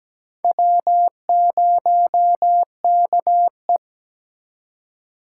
Morse code, heard as a keyed tone, about 20 wpm, 700 Hz. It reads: W0KE
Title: Speed Racing - USA 1x2 Call Signs 17wpm